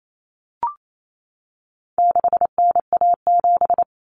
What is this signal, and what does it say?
Morse code, heard as a keyed tone, about 28 wpm, 700 Hz. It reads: 6NA7
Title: Common Cut Numbers - 4-Digit 28wpm